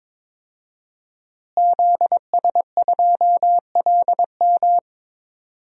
Morse code, heard as a keyed tone, about 22 wpm, 700 Hz. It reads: ZS2LM
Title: DXCC-100 Call Signs - 5 Characters - Repeated 3x - 22wpm